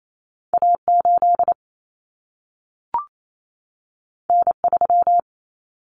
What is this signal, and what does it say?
Morse code, heard as a keyed tone, about 28 wpm, 700 Hz. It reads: A8 N3
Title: Common Cut Numbers - 2-Digit 28wpm